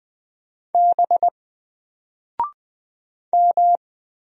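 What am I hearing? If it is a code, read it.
Morse code, heard as a keyed tone, about 20 wpm, 700 Hz. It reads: B M
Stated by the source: Single letter and number Morse drill at 20 wpm, from N4LCD.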